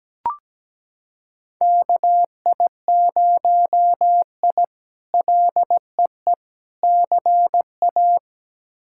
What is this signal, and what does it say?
Morse code, heard as a keyed tone, about 17 wpm, 700 Hz. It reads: KI0I LEE CA
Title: K1USN SST QSO Practice - 17wpm